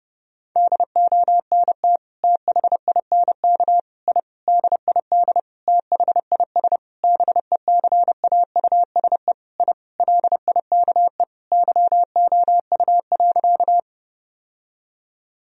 Morse code, heard as a keyed tone, about 30 wpm, 700 Hz. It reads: DONT THINK I DID THIS BECAUSE I LIKE YOU.
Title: Sentences from Top 1000 Words - Review Part 3 - 30wpm